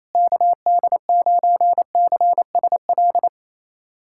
Morse code, heard as a keyed tone, about 28 wpm, 700 Hz. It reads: KD9CSL
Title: USA Callsign DE Callsign - 6 Characters - 28wpm